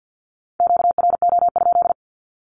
Morse code, heard as a keyed tone, about 50 wpm, 700 Hz. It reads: QRO?
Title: Q-Codes with Spoken Meaning - 50wpm